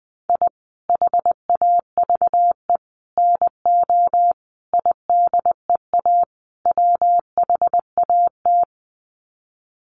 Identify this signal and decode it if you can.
Morse code, heard as a keyed tone, about 20 wpm, 700 Hz. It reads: I HAVE NO IDEA WHAT
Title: Sets of 5 Words Encore 20wpm